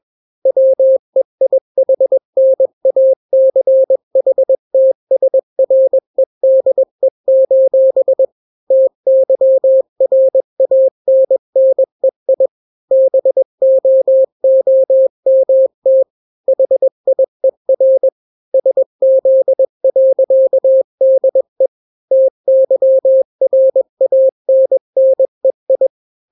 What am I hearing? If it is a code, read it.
Morse code, heard as a keyed tone, about 21 wpm, 530 Hz. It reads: WEIHNACHTSREDE: TYRANNEI BOOMT HIER SZ.DE TYRANNEI
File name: daily / 20wpm